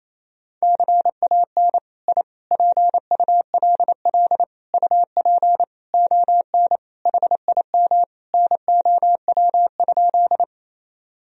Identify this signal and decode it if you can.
Morse code, heard as a keyed tone, about 28 wpm, 700 Hz. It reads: CAN I PULL UP ON HIM NOW?
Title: Sentences from Top 1000 Words - Review Part 2 - 28wpm